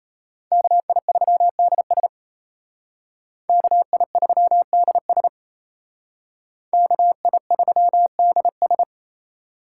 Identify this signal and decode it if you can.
Morse code, heard as a keyed tone, about 32 wpm, 700 Hz. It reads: KI3DS KI3DS KI3DS
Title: Speed Racing - USA 2x2 Call Signs 28wpm